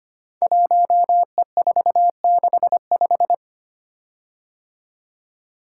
Morse code, heard as a keyed tone, about 25 wpm, 700 Hz. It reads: 1E465
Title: Code Groups - 5 Letter:Numbers - 25wpm